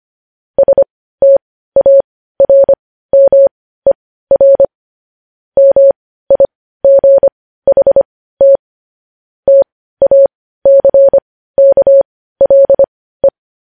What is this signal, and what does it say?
Morse code, heard as a keyed tone, about 25 wpm, spaced as for 15 wpm, 550 Hz. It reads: STARMER MIGHT TACKLE
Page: news-headlines-in-morse-code-at-15-wpm